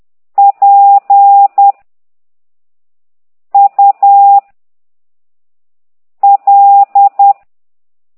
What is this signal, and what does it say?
Morse code, heard as a keyed tone, about 10 wpm, 800 Hz. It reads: P U L
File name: ct2idw / exerc9